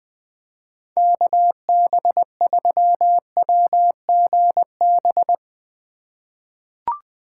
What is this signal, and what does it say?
Morse code, heard as a keyed tone, about 20 wpm, 700 Hz. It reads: KB3WGB
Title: USA Call Signs - 6 Characters - 8x Character Spacing - 20wpm